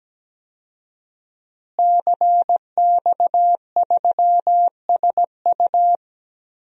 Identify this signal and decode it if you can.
Morse code, heard as a keyed tone, about 17 wpm, 700 Hz. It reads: CX3SU
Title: DXCC-100 Call Signs - 5 Characters - Repeated 3x - 17wpm